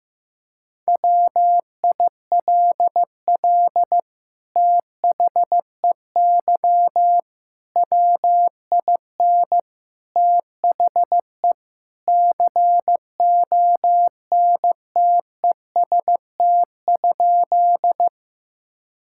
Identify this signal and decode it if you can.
Morse code, heard as a keyed tone, about 15 wpm, 700 Hz. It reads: WILL THEY WIN THE CONTEST?
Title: Sentences from Top 1600 Words 15wpm